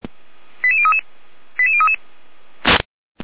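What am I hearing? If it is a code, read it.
Five-tone selective call (ZVEI), heard as quick runs of five tones, tones of about 80 ms. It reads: 88033 88033
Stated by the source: ZVEI five-tone selective call received on SDR.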